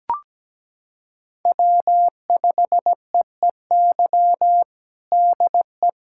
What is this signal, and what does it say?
Morse code, heard as a keyed tone, about 17 wpm, 700 Hz. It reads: W5EEY DE
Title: USA Callsign DE Callsign - 5 Characters - 17wpm